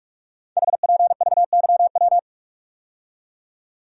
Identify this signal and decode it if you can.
Morse code, heard as a keyed tone, about 44 wpm, 700 Hz. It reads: HPVYW
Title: Code Groups - 5 Letters - 45wpm